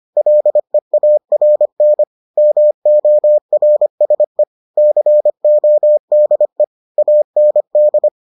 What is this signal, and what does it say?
Morse code, heard as a keyed tone, about 25 wpm, 600 Hz. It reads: LEARN MORSE CODE AND